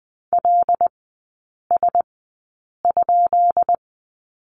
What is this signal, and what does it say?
Morse code, heard as a keyed tone, about 20 wpm, spaced as for 8 wpm, 700 Hz. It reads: LS?
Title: 178 Lesson 29e - Slash - Three Characters 20wpm